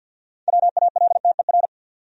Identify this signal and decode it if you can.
Morse code, heard as a keyed tone, about 52 wpm, 700 Hz. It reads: WALTER
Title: USA Common Names - 50wpm